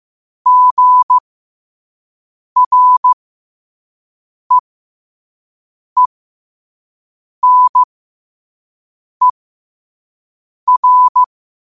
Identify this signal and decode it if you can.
Morse code, heard as a keyed tone, about 15 wpm, spaced as for 5 wpm, 1000 Hz. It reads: GREENER